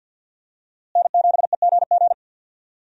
Morse code, heard as a keyed tone, about 53 wpm, 700 Hz. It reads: N7EGG
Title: USA Call Signs - 5 Characters - Repeated 3x - 50wpm